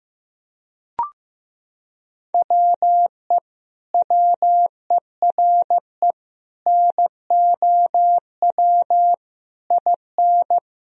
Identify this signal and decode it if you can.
Morse code, heard as a keyed tone, about 15 wpm, 700 Hz. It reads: WE WERE NOW IN
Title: Sentences from Top 1000 Words - Review Part 2 - 15wpm